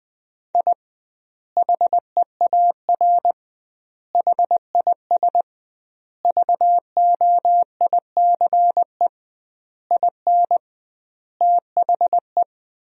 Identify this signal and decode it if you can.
Morse code, heard as a keyed tone, about 21 wpm, 700 Hz. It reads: I HEAR HIS VOICE IN THE
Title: 2x Word-Spacing - Sentences from Top 500 Words - 20wpm